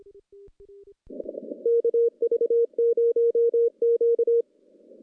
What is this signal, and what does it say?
Morse code, heard as a keyed tone, about 26 wpm, 470 Hz. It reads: KV0Q
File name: contests / CW-SS-k5tr- -1300